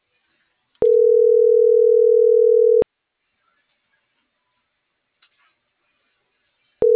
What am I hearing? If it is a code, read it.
Ringback tone (North America), 440 + 480 Hz.